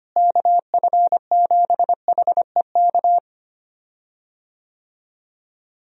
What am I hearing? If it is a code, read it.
Morse code, heard as a keyed tone, about 25 wpm, 700 Hz. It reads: KF7HEK
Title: USA Call Signs - 6 Characters - Repeated 3x - 25wpm